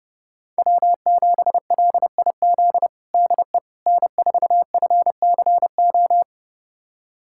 Morse code, heard as a keyed tone, about 30 wpm, 700 Hz. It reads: W7LIZ DE N4FCO
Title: USA Callsign DE Callsign - 5 Characters - 30wpm